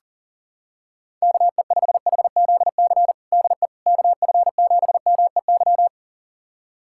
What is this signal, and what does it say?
Morse code, heard as a keyed tone, about 40 wpm, 700 Hz. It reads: KE5HZC DE KF7MEY
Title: USA Callsign DE Callsign - 6 Characters - 40wpm